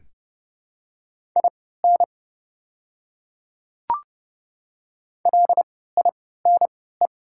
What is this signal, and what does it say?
Morse code, heard as a keyed tone, about 30 wpm, spaced as for 17 wpm, 700 Hz. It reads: IN LINE
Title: 3x Character Spacing - Top 200 Words - 30wpm